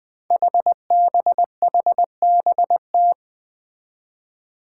Morse code, heard as a keyed tone, about 20 wpm, 700 Hz. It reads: HBHBT
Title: Five Challenging Letter-Number Combinations 20wpm